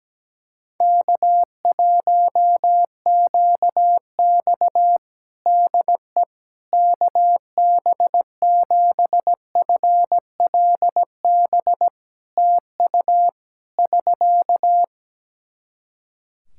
Morse code, heard as a keyed tone, about 17 wpm, 700 Hz. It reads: K1QX DE KB7FLB TU <SK>
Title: Practice QSO - Exchange 4 - Ending - 17wpm